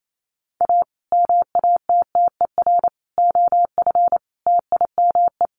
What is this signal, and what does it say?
Morse code, heard as a keyed tone, about 28 wpm, 700 Hz. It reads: A MATTER OF TIME
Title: Sets of 5 Words Encore 28wpm